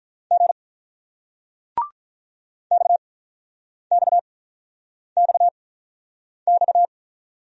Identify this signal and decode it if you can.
Morse code, heard as a keyed tone, about 47 wpm, spaced as for 9 wpm, 700 Hz. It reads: G XXXX